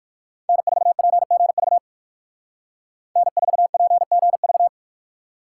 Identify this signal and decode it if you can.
Morse code, heard as a keyed tone, about 53 wpm, 700 Hz. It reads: N4PGV N4PGV
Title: Speed Racing - USA 1x3 Call Signs 45wpm